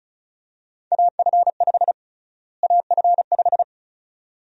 Morse code, heard as a keyed tone, about 36 wpm, 700 Hz. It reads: AF5 AF5
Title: USA Call Sign Prefixes - Repeated 2x - 35wpm